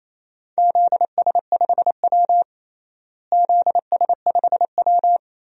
Morse code, heard as a keyed tone, about 28 wpm, 700 Hz. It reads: ZS5W ZS5W
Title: DXCC-100 Call Signs - 4 Characters - Repeated 2x - 28wpm